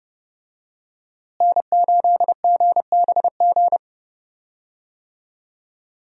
Morse code, heard as a keyed tone, about 30 wpm, 700 Hz. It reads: N8GBG
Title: USA Call Signs - 5 Characters - Repeated 3x - 30wpm